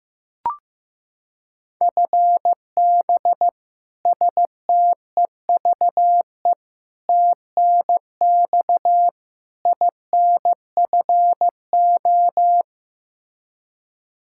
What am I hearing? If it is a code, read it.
Morse code, heard as a keyed tone, about 15 wpm, 700 Hz. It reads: FB STEVE TNX INFO
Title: Practice QSO - Exchange 3 - More Info - 15wpm